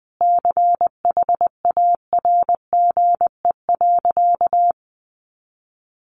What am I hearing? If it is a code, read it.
Morse code, heard as a keyed tone, about 20 wpm, 700 Hz. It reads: CHARGE.
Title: Sentences from Top 1300 Words 20wpm